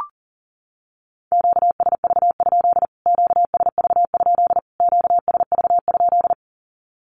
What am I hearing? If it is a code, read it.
Morse code, heard as a keyed tone, about 40 wpm, 700 Hz. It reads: QSV? QSV? QSV?
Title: Q-Codes Repeated 3x 40wpm